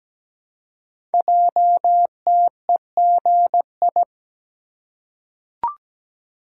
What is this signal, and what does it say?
Morse code, heard as a keyed tone, about 17 wpm, 700 Hz. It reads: JTEGI
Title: Code Groups - 5 Letters - 17wpm